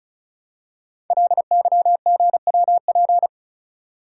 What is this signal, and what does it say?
Morse code, heard as a keyed tone, about 36 wpm, 700 Hz. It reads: LYGWP
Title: Code Groups - 5 Letter:Numbers - 35wpm